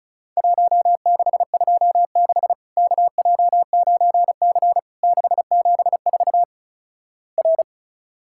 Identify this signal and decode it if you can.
Morse code, heard as a keyed tone, about 36 wpm, 700 Hz. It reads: 1626 KJ9C 674 R
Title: Worked All Europe DX Contest - QTC Exchange - 35wpm